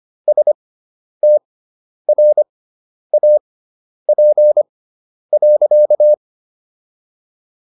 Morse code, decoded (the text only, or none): STRAP.